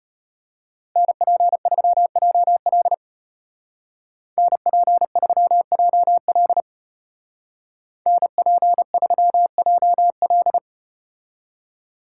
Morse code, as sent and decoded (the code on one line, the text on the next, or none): -. .--. ...-- .--- .-.. / -. .--. ...-- .--- .-.. / -. .--. ...-- .--- .-..
NP3JL NP3JL NP3JL